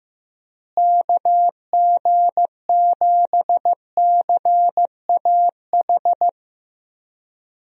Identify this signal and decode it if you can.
Morse code, heard as a keyed tone, about 15 wpm, 700 Hz. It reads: KG7CAH
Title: USA Call Signs - 6 Characters - Repeated 2x - 15wpm